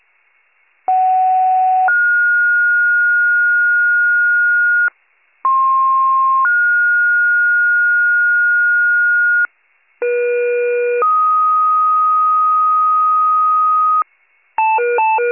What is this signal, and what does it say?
Two-tone page, heard as a one-second tone, then a three-second tone, about 750 then 1430 Hz, about 1060 then 1430 Hz, about 480 then 1250 Hz, with noise. It is heard from 0.9 to 14.0 s.